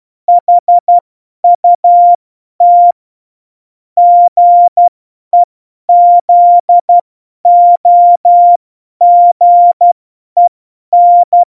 Morse code, heard as a keyed tone, about 12 wpm, spaced as for 10 wpm, 700 Hz. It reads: HUT GEZOGEN
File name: Morsekurs / Witze / Witz 03 April 2018, 50 F70